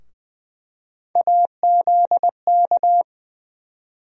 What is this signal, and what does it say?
Morse code, heard as a keyed tone, about 20 wpm, 700 Hz. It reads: AZK
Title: USA Call Sign Postfixes - 3 Letters 20wpm